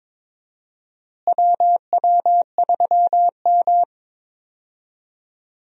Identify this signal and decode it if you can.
Morse code, heard as a keyed tone, about 21 wpm, 700 Hz. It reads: WW3M
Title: USA Call Signs - 4 Characters - Repeated 2x - 22wpm